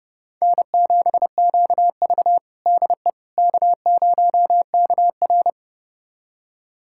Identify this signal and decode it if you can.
Morse code, heard as a keyed tone, about 32 wpm, 700 Hz. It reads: N7QV DE K0KR
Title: USA Callsign DE Callsign - 4 Characters - 30wpm